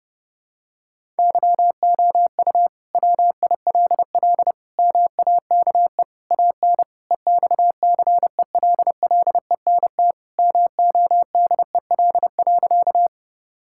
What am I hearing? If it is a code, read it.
Morse code, heard as a keyed tone, about 32 wpm, 700 Hz. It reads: YOU WILL MAKE AN EXCELLENT MODEL.
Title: Sentences from Top 1000 Words - Review Part 5 - 30wpm